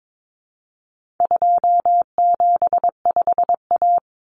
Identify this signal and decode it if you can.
Morse code, heard as a keyed tone, about 22 wpm, 700 Hz. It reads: 275A
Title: Common Cut Numbers - 4-Digit 22wpm